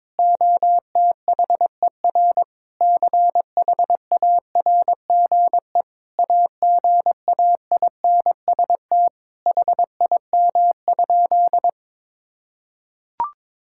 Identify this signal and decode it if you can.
Morse code, heard as a keyed tone, about 22 wpm, 700 Hz. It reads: OTHER CHARGE AGAINST HIM?
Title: Sentences from Top 1000 Words - Review Part 6 - 22wpm